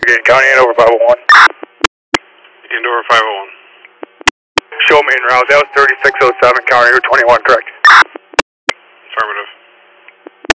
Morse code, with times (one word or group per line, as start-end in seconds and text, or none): none